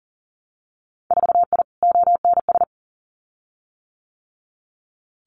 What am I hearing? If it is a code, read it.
Morse code, heard as a keyed tone, about 38 wpm, 700 Hz. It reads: 4I ONS
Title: Winter Field Day Exchanges Encore 40wpm